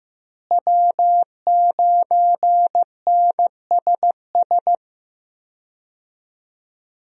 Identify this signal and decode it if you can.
Morse code, heard as a keyed tone, about 15 wpm, 700 Hz. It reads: W9NSS
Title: USA Call Signs - 5 Characters - Repeated 3x - 15wpm